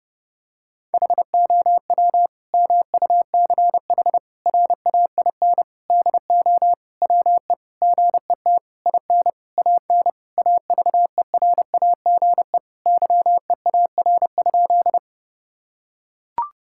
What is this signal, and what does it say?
Morse code, heard as a keyed tone, about 30 wpm, 700 Hz. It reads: HOW MUCH RAIN DO WE GET IN AN AVERAGE YEAR?
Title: Sentences from Top 1000 Words - Review Part 6 - 30wpm